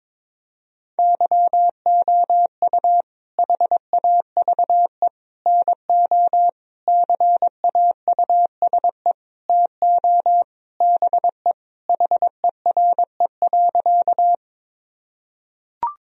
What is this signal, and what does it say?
Morse code, heard as a keyed tone, about 22 wpm, 700 Hz. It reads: YOU HAVE NO CAUSE TO BE HERE.
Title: Sentences from Top 1000 Words - Review Part 1 - 22wpm